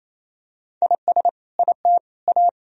Morse code, heard as a keyed tone, about 28 wpm, 700 Hz. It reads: IS IT A
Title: Sentences from Top 1000 Words - Review Part 4 - 28wpm